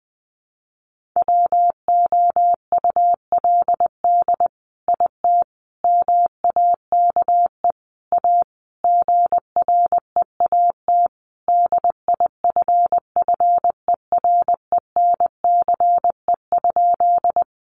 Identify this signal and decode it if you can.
Morse code, heard as a keyed tone, about 20 wpm, 700 Hz. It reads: WOULD IT MAKE A GREAT DIFFERENCE?
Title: Sentences from Top 800 Words 20wpm